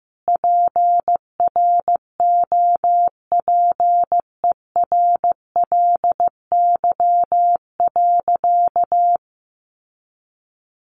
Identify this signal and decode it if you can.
Morse code, heard as a keyed tone, about 14 wpm, 700 Hz. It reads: PROPERLY.
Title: Sentences from Top 1100 Words 15wpm